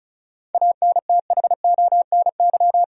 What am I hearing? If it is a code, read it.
Morse code, heard as a keyed tone, about 34 wpm, 700 Hz. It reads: ANTHONY
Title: USA Common Names - 35wpm